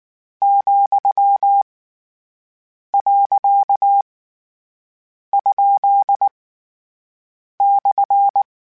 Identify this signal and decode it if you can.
Morse code, heard as a keyed tone, about 19 wpm, 800 Hz. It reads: , . ? /
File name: Misc / Misc-mp3s / Words @18